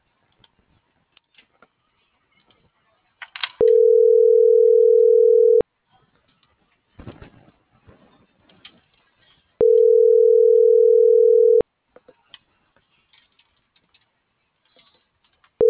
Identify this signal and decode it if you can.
Ringback tone (North America), 440 + 480 Hz.